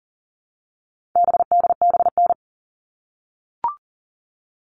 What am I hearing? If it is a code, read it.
Morse code, heard as a keyed tone, about 40 wpm, 700 Hz. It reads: BDBN
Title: All Cut Numbers - 4-Digit 40wpm